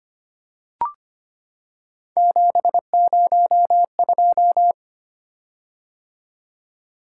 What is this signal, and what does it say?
Morse code, heard as a keyed tone, about 25 wpm, 700 Hz. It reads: 702
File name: Three Letter-Number Combinations - 25wpm